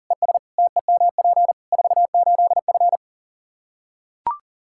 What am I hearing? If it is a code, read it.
Morse code, heard as a keyed tone, about 37 wpm, 700 Hz. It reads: ES TEMP 48F
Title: Practice QSO - Exchange 2 - Info - 40wpm